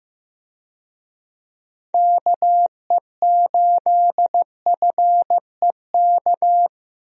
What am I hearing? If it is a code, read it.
Morse code, heard as a keyed tone, about 15 wpm, 700 Hz. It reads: KE8FEK
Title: USA Call Signs - 6 Characters - Repeated 3x - 15wpm